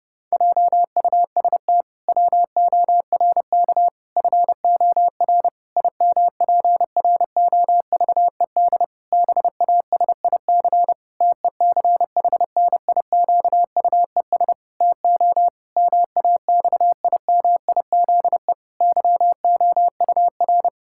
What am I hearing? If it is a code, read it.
Morse code, heard as a keyed tone, about 32 wpm, 700 Hz. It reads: JUST WORK FOR IMPROVED BASIC TECHNIQUES TO MAXIMIZE YOUR
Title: Pangram Challenge v2 - 30wpm